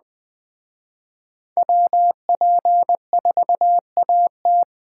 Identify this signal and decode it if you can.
Morse code, heard as a keyed tone, about 20 wpm, 700 Hz. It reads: WP4AT